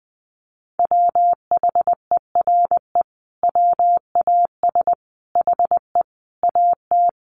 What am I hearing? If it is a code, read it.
Morse code, heard as a keyed tone, about 20 wpm, 700 Hz. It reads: WHERE WAS HE AT
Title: Sentences from Top 1000 Words 20wpm